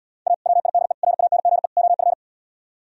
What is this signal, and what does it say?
Morse code, heard as a keyed tone, about 80 wpm, 690 Hz. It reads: A WIDE RANGE OF